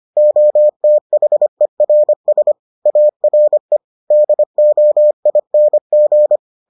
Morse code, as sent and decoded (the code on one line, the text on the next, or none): --- - .... .-. ... / .- .-. / -.. --- .. -. --.
OTHERS ARE DOING